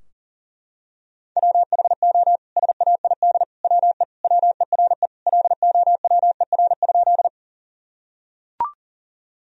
Morse code, heard as a keyed tone, about 40 wpm, 700 Hz. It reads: WHO SAID WE WERE LOWER?
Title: Sentences from Top 1000 Words - Review Part 4 - 40wpm